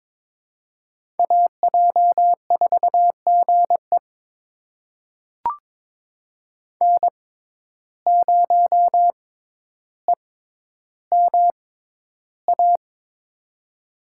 Morse code, heard as a keyed tone, about 22 wpm, spaced as for 11 wpm, 700 Hz. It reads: AJ4GE N0EMA
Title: USA Call Signs - 5 Characters - 6x Character Spacing - 22wpm